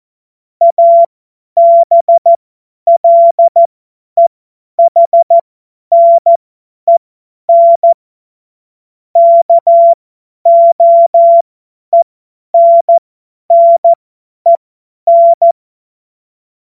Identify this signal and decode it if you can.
Morse code, heard as a keyed tone, about 14 wpm, spaced as for 10 wpm, 680 Hz. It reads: ABLEHNEN KOENNEN
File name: Morsekurs / Witze / Witz 14.November 2023 50 F70